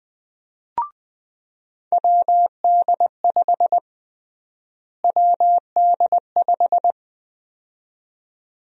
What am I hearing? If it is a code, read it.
Morse code, heard as a keyed tone, about 21 wpm, 700 Hz. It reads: WD5 WD5
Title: USA Call Sign Prefixes - Repeated 2x - 20wpm